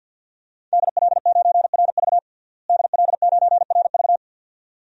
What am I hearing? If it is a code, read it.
Morse code, heard as a keyed tone, about 52 wpm, 700 Hz. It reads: DL9RV DL9RV
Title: DXCC-100 Call Signs - 5 Characters - Repeated 2x - 50wpm